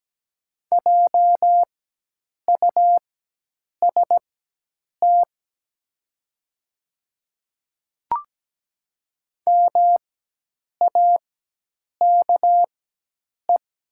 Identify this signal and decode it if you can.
Morse code, heard as a keyed tone, about 17 wpm, spaced as for 8 wpm, 700 Hz. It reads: JUST MAKE
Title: 4x Character Spacing - Top 300 Words - 17wpm